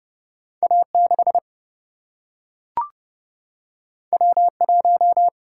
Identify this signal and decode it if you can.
Morse code, heard as a keyed tone, about 30 wpm, 700 Hz. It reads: A6 W1
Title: Single-Letter Single-Number 30wpm